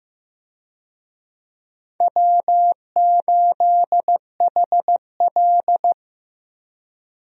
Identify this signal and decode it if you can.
Morse code, heard as a keyed tone, about 15 wpm, 700 Hz. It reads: W8HL
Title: USA Call Signs - 4 Characters - Repeated 2x - 15wpm